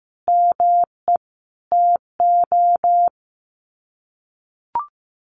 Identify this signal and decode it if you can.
Morse code, heard as a keyed tone, about 15 wpm, 700 Hz. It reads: ME TO
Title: Sets of 5 Words 15wpm